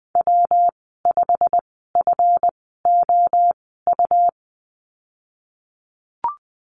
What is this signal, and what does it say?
Morse code, heard as a keyed tone, about 19 wpm, spaced as for 14 wpm, 700 Hz. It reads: W5FOU
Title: 193 Lesson 31g - Letter v - Call Signs 20wpm